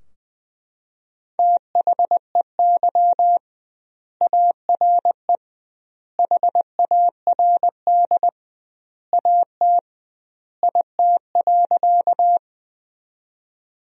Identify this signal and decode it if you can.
Morse code, heard as a keyed tone, about 20 wpm, 700 Hz. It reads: THEY ARE HARD AT IT.
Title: 2x Word-Spacing - Sentences from Top 300 Words - 20wpm